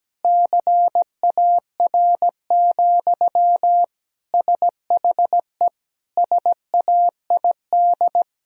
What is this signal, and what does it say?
Morse code, heard as a keyed tone, about 17 wpm, 700 Hz. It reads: CAR, SHE SAID
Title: Sentences from Top 1000 Words - Review Part 1 - 17wpm